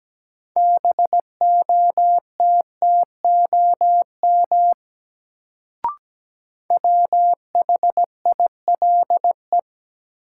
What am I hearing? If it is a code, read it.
Morse code, heard as a keyed tone, about 17 wpm, 700 Hz. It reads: BOTTOM WHILE
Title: Sets of 3 Words - New Word Drill - 17wpm